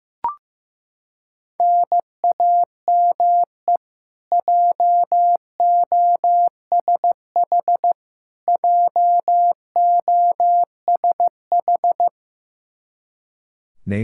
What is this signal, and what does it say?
Morse code, heard as a keyed tone, about 15 wpm, 700 Hz. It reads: NAME JOSH JOSH
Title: Practice QSO - Exchange 1 - Beginning - 15wpm